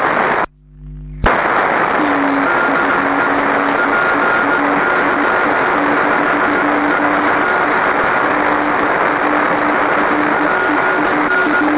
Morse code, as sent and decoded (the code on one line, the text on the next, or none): none